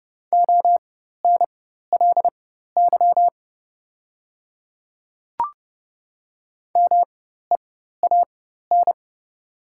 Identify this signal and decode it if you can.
Morse code, heard as a keyed tone, about 30 wpm, spaced as for 14 wpm, 700 Hz. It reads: ONLY MEAN
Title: 4x Character Spacing - Top 500 Words - 30wpm